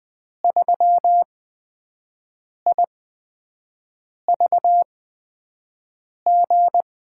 Morse code, heard as a keyed tone, about 20 wpm, spaced as for 5 wpm, 700 Hz. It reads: 3IVG
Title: USA Call Signs - 8x Character Spacing - 20wpm